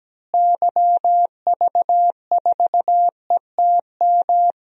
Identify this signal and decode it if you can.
Morse code, heard as a keyed tone, about 17 wpm, 700 Hz. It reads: YV4ETM
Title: DXCC-100 Call Signs - 6 Characters - Repeated 3x - 17wpm